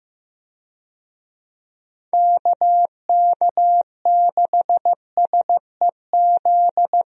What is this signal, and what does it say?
Morse code, heard as a keyed tone, about 15 wpm, 700 Hz. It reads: KK6SEZ
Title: USA Call Signs - 6 Characters - Repeated 2x - 15wpm